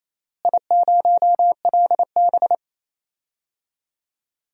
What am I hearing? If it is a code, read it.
Morse code, heard as a keyed tone, about 27 wpm, 700 Hz. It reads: I0LB
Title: DXCC-100 Call Signs - 4 Characters - Repeated 3x - 28wpm